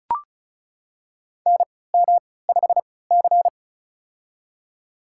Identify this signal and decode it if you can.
Morse code, heard as a keyed tone, about 34 wpm, spaced as for 20 wpm, 700 Hz. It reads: NM5C
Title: USA Call Signs - 4 Characters - 3x Character Spacing - 35wpm